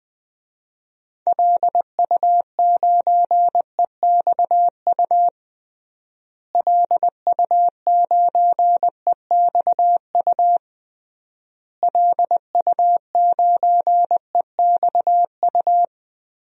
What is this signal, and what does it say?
Morse code, heard as a keyed tone, about 20 wpm, 700 Hz. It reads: LU9EXU LU9EXU LU9EXU
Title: DXCC-100 Call Signs - 6 Characters - Repeated 3x - 20wpm